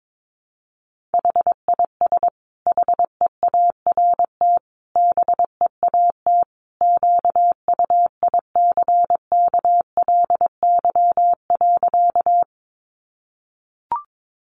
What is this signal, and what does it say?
Morse code, heard as a keyed tone, about 22 wpm, 700 Hz. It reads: HIS HEART BEAT QUICKLY.
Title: Sentences from Top 1300 Words 22wpm